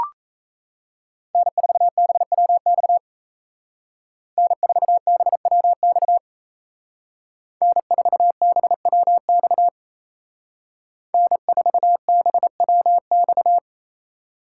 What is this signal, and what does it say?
Morse code, heard as a keyed tone, about 36 wpm, 700 Hz. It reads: N4BWX N4BWX N4BWX N4BWX